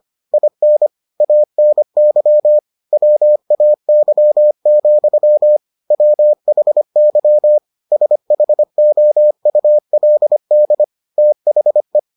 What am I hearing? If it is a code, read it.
Morse code, heard as a keyed tone, about 25 wpm, 600 Hz. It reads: IN ANY WAY, WHY SHOULD THE